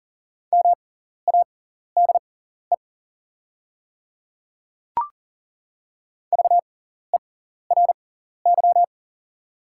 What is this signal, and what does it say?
Morse code, heard as a keyed tone, about 40 wpm, spaced as for 14 wpm, 690 Hz. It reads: MADE VERY
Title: 6x Character Spacing - Top 200 Words - 40wpm